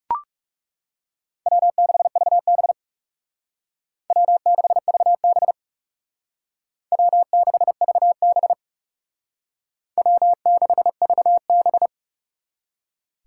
Morse code, heard as a keyed tone, about 38 wpm, 700 Hz. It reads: W6VB W6VB W6VB W6VB